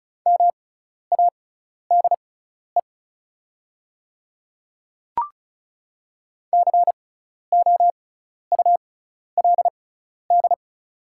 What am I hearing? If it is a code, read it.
Morse code, heard as a keyed tone, about 36 wpm, spaced as for 12 wpm, 700 Hz. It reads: MADE COULD
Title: 6x Character Spacing - Top 300 Words - 35wpm